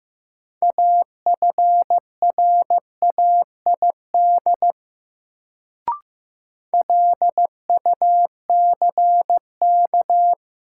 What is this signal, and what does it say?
Morse code, heard as a keyed tone, about 15 wpm, 700 Hz. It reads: AFRAID LUCK
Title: Sentences from Top 1500 Words - New Word Drill - 15wpm